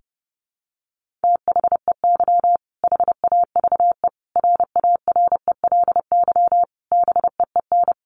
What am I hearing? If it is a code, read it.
Morse code, heard as a keyed tone, about 30 wpm, 700 Hz. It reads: THEY HAVE RARELY BEEN
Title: Sentences from Top 1400 Words 30wpm